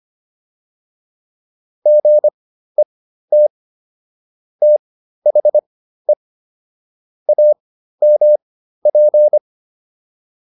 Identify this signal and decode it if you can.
Morse code, heard as a keyed tone, about 25 wpm, spaced as for 13 wpm, 600 Hz. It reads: GET THE AMP